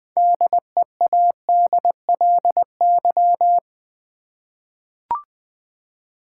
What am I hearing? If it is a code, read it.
Morse code, heard as a keyed tone, about 20 wpm, 700 Hz. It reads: DEADLY
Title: 6 Letter Words - First 3 Letters Given - 20wpm